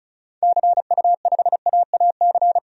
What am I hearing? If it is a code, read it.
Morse code, heard as a keyed tone, about 36 wpm, 700 Hz. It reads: CU5AAC
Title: DXCC-100 Call Signs - 6 Characters - 35wpm